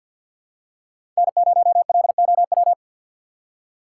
Morse code, heard as a keyed tone, about 51 wpm, 700 Hz. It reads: N0LOW
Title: USA Call Signs - 5 Characters - 4x Character Spacing - 50wpm